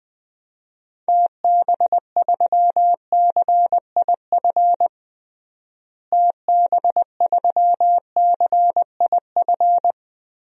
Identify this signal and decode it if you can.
Morse code, heard as a keyed tone, about 21 wpm, 700 Hz. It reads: TB3CIF TB3CIF
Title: DXCC-100 Call Signs - 6 Characters - Repeated 2x - 20wpm